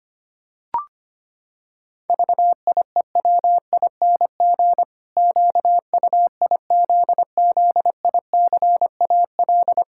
Morse code, heard as a keyed tone, about 25 wpm, 700 Hz. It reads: VIEWING QUIZZICAL
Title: Pangram Challenge v2 - 25wpm